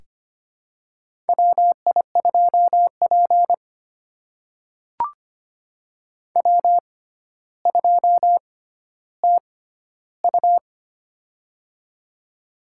Morse code, heard as a keyed tone, about 25 wpm, spaced as for 13 wpm, 700 Hz. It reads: WI2P W2TU
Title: USA Call Signs - 4 Characters - 6x Character Spacing - 25wpm